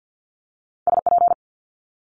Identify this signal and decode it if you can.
Morse code, heard as a keyed tone, about 50 wpm, 700 Hz. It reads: SP
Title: Multi-letter Phonemes Quick Fire 50wpm